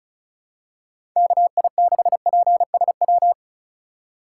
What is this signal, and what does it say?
Morse code, heard as a keyed tone, about 36 wpm, 700 Hz. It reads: KI6PSW
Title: USA Call Signs - 4x Character Spacing - 35wpm